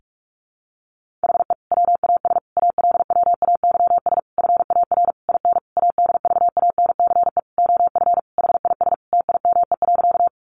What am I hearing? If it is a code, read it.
Morse code, heard as a keyed tone, about 44 wpm, 700 Hz. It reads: HE WAS ALWAYS FAR IN ADVANCE OF HIS TIME.
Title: Sentences from Top 1400 Words 45wpm